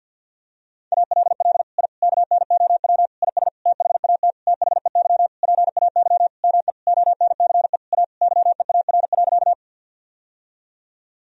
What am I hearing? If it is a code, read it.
Morse code, heard as a keyed tone, about 52 wpm, 700 Hz. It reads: ALL I KNOW IS THAT THEY PAY ME ONCE A YEAR.
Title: Sentences from Top 1000 Words - Review Part 1 - 50wpm